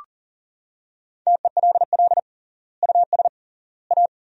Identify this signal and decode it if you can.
Morse code, heard as a keyed tone, about 40 wpm, 700 Hz. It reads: TELL US A
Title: 3x Word-Spacing - Sentences from Top 300 Words - 40wpm